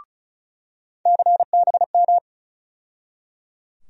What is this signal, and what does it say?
Morse code, heard as a keyed tone, about 36 wpm, 700 Hz. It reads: CBM
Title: USA Call Sign Postfixes - 3 Letters 35wpm